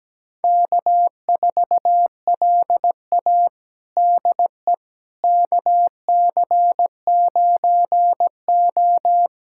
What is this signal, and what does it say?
Morse code, heard as a keyed tone, about 17 wpm, 700 Hz. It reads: K4LA DE KC9O
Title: USA Callsign DE Callsign - 4 Characters - 17wpm